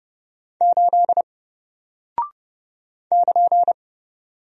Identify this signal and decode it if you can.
Morse code, heard as a keyed tone, about 30 wpm, 700 Hz. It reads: 8 <KN>
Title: Single Character (Letter:Number:Punctuation:Prosign) Alt 30wpm